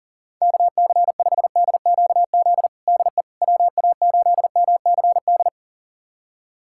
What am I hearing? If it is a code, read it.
Morse code, heard as a keyed tone, about 40 wpm, 700 Hz. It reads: KC5DQZ DE WA8MCD
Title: USA Callsign DE Callsign 40wpm